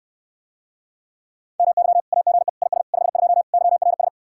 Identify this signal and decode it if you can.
Morse code, heard as a keyed tone, about 71 wpm, 700 Hz. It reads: MY AGE IS 52 YRS